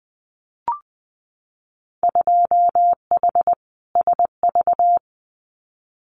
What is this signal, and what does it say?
Morse code, heard as a keyed tone, about 20 wpm, 700 Hz. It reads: 2H SV
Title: Winter Field Day Exchanges Encore 20wpm